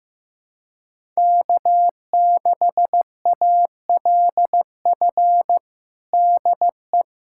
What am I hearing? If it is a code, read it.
Morse code, heard as a keyed tone, about 15 wpm, 700 Hz. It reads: K6ALF DE
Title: USA Callsign DE Callsign - 5 Characters - 15wpm